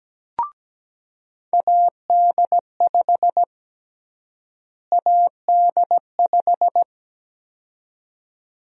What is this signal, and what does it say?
Morse code, heard as a keyed tone, about 17 wpm, 700 Hz. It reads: AD5 AD5
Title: USA Call Sign Prefixes - Repeated 2x - 17wpm